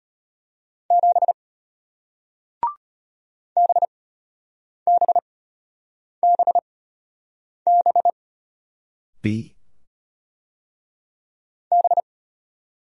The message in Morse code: 7 BBBB B

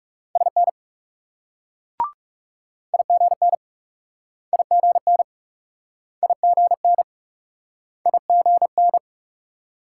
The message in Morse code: SN IGN IGN IGN IGN

